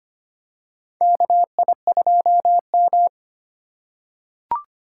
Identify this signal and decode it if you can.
Morse code, heard as a keyed tone, about 25 wpm, 700 Hz. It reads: KI2M
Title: USA Call Signs - 4 Characters - 8x Character Spacing - 25wpm